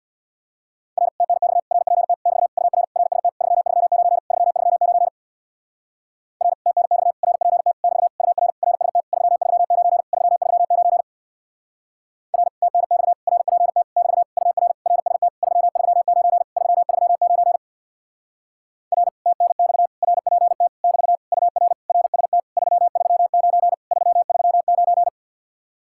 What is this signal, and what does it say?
Morse code, heard as a keyed tone, about 65 wpm, 700 Hz. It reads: R TNX RPT <BT> UR RST 339 339 R TNX RPT <BT> UR RST 339 339 R TNX RPT <BT> UR RST 339 339 R TNX RPT <BT> UR RST 339 339